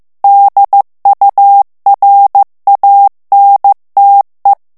Morse code, heard as a keyed tone, about 15 wpm, 800 Hz. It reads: DURANTE
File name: ITALIEN texte clair 1 vitesse 14 mots minute